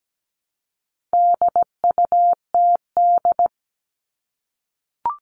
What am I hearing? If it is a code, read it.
Morse code, heard as a keyed tone, about 17 wpm, 700 Hz. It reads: DUTD
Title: All Cut Numbers - 4-Digit 17wpm